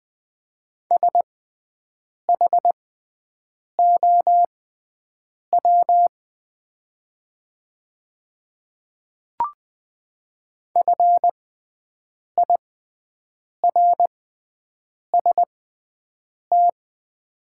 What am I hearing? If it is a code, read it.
Morse code, heard as a keyed tone, about 20 wpm, spaced as for 7 wpm, 700 Hz. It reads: SHOW FIRST
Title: 6x Character Spacing - Top 200 Words - 20wpm